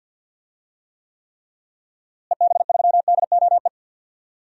Morse code, heard as a keyed tone, about 49 wpm, 700 Hz. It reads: EB3DOE
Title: DXCC-100 Call Signs - 6 Characters - Repeated 3x - 50wpm